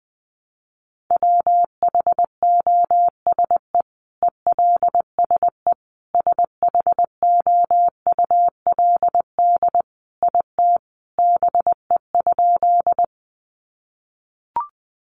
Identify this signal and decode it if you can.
Morse code, heard as a keyed tone, about 20 wpm, 700 Hz. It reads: WHOSE ELSE SHOULD IT BE?
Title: Sentences from Top 1200 Words 20wpm